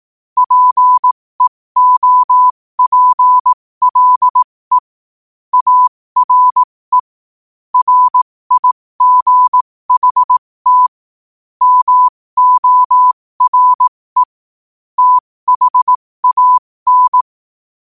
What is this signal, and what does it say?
Morse code, heard as a keyed tone, about 18 wpm, spaced as for 15 wpm, 1000 Hz. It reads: PEOPLE ARE RIGHT MORE THAN
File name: qotdfiles / 15wpm